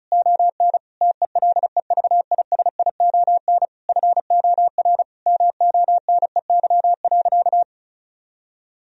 Morse code, heard as a keyed tone, about 36 wpm, 700 Hz. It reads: ON TELEVISION FOR MONEY.
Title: Sentences from Top 1000 Words - Review Part 4 - 35wpm